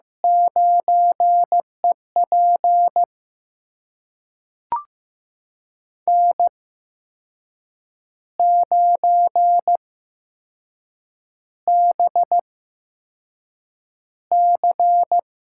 Morse code, heard as a keyed tone, about 15 wpm, spaced as for 6 wpm, 700 Hz. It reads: 9EP N9BC
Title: USA Call Signs - 5 Characters - 8x Character Spacing - 15wpm